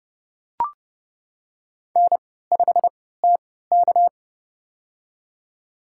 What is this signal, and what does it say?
Morse code, heard as a keyed tone, about 30 wpm, spaced as for 17 wpm, 700 Hz. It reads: N5TK
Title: USA Call Signs - 4 Characters - 3x Character Spacing - 30wpm